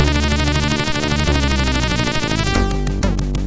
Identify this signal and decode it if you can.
There is no code.